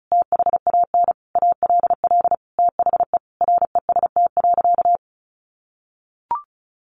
Morse code, heard as a keyed tone, about 34 wpm, 700 Hz. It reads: THAN ALL THE REST.
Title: Sentences from Top 1300 Words 35wpm